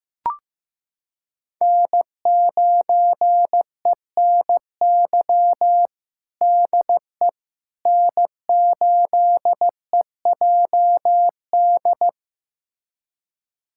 Morse code, heard as a keyed tone, about 15 wpm, 700 Hz. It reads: N9ENY DE N8EJD
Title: USA Callsign DE Callsign - 5 Characters - 15wpm